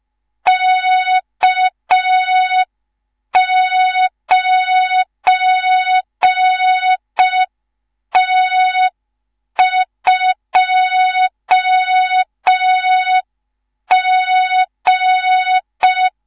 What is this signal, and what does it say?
Morse code, heard as a keyed tone, about 5 wpm, 3700 Hz. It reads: K9T2G
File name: Morze-HA2MN / hang / 19 GYAKORLAT